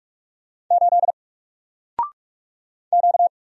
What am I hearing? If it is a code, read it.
Morse code, heard as a keyed tone, about 45 wpm, 700 Hz. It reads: 8 Q
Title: Single Letter-Number Alt 45wpm